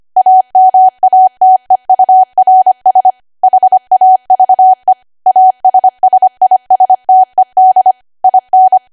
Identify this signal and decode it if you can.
Morse code, heard as a keyed tone, about 25 wpm, 750 Hz. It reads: AMATEURS HAVE ASSISTED IN
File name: Archive / 25 WPM / 25WPM